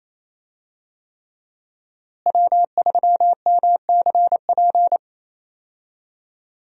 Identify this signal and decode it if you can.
Morse code, heard as a keyed tone, about 28 wpm, 700 Hz. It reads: W3MCP
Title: USA Call Signs - 5 Characters - Repeated 3x - 28wpm